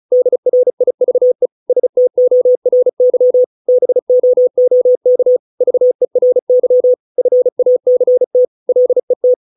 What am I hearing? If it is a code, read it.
Morse code, heard as a keyed tone, about 34 wpm, 500 Hz. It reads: DRIVE STORY BOOK VERY FACT LET